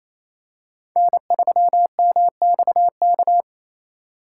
Morse code, heard as a keyed tone, about 28 wpm, 700 Hz. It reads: N3MXK
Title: USA Call Signs - 5 Characters - 4x Character Spacing - 28wpm